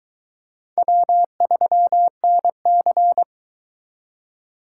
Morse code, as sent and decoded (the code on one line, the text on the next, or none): .-- ...-- -. -.-.
W3NC